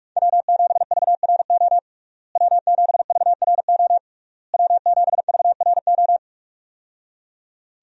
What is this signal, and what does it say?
Morse code, heard as a keyed tone, about 41 wpm, 700 Hz. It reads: W7VRO W7VRO W7VRO
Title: USA Call Signs - 5 Characters - Repeated 3x - 45wpm